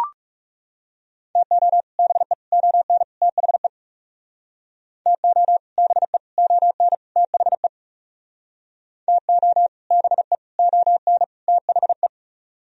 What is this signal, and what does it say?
Morse code, heard as a keyed tone, about 39 wpm, 700 Hz. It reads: TO BE ON THE TO BE ON THE TO BE ON THE